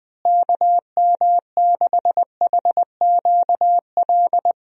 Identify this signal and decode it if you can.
Morse code, heard as a keyed tone, about 21 wpm, 700 Hz. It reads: KM6HQL
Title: USA Call Signs - 6 Characters - Repeated 3x - 20wpm